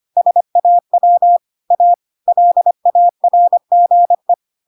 Morse code, heard as a keyed tone, about 25 wpm, 700 Hz. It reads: SAW A LARGE
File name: Pinocchio / Pinocchio ch4 25 25 ik1hge